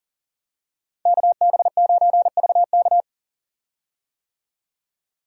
Morse code, heard as a keyed tone, about 40 wpm, 700 Hz. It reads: KB9VK
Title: USA Call Signs - 5 Characters - 40wpm